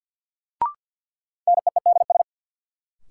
Morse code, heard as a keyed tone, about 53 wpm, 700 Hz. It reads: NEEDS
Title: Sentences from Top 1600 Words - New Word Drill Alt - 50wpm